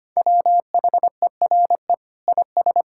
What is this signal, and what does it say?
Morse code, heard as a keyed tone, about 25 wpm, 700 Hz. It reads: WHERE IS
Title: Sentences from Top 1600 Words 25wpm